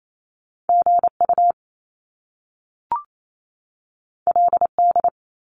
Morse code, heard as a keyed tone, about 28 wpm, 700 Hz. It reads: GU LD
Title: Top Two Letter Combinations in Words 28wpm